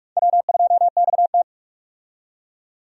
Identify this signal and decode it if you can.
Morse code, heard as a keyed tone, about 46 wpm, 700 Hz. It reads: W2XT
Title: USA Call Signs - 4 Characters - 45wpm